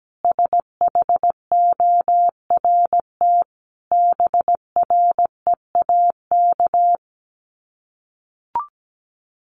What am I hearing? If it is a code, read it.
Morse code, heard as a keyed tone, about 17 wpm, 700 Hz. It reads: SHORT BREAK
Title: Sets of 5 Words Encore2 17wpm